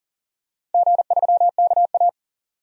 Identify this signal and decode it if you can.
Morse code, heard as a keyed tone, about 40 wpm, 690 Hz. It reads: G3KA
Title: DXCC-100 Call Signs - 4 Characters - 40wpm